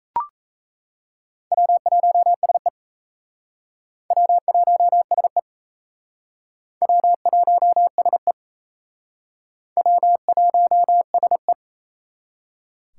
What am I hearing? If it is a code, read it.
Morse code, heard as a keyed tone, about 36 wpm, 700 Hz. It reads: W1SE W1SE W1SE W1SE